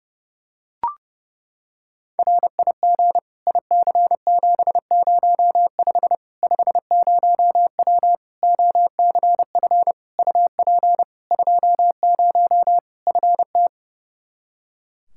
Morse code, heard as a keyed tone, about 30 wpm, 700 Hz. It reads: RIG IC705 50W OCF UP 20 FT
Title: Practice QSO - Exchange 2 - Info - 30wpm